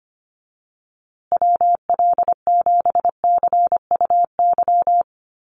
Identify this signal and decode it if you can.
Morse code, heard as a keyed tone, about 25 wpm, 700 Hz. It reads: WL7CUY
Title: USA Call Signs Repeated 2x 25wpm